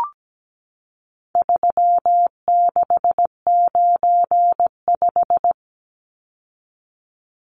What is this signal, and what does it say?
Morse code, heard as a keyed tone, about 17 wpm, 700 Hz. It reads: 3695
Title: Numbers - 4-Digit 17wpm